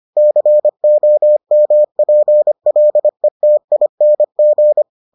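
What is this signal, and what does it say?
Morse code, heard as a keyed tone, about 25 wpm, 600 Hz. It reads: COMPLETING
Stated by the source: Morse edition of the Foundations of Amateur Radio episodes.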